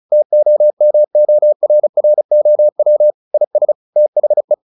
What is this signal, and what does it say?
Morse code, heard as a keyed tone, about 34 wpm, 600 Hz. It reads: TOMORROW IS THE
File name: the old man and the sea / the old man and the sea 1 35wpm